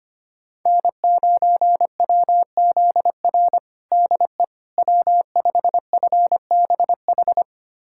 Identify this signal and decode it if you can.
Morse code, heard as a keyed tone, about 25 wpm, 700 Hz. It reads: N9WZR DE W5FBH
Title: USA Callsign DE Callsign - 5 Characters - 25wpm